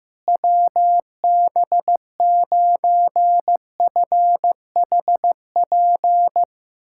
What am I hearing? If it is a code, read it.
Morse code, heard as a keyed tone, about 15 wpm, 700 Hz. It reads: WB9FHP
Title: USA Callsign DE Callsign - 6 Characters - 15wpm